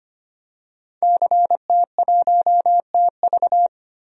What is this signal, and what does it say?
Morse code, heard as a keyed tone, about 25 wpm, 700 Hz. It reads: CT1TV
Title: DXCC-100 Call Signs - 5 Characters - 25wpm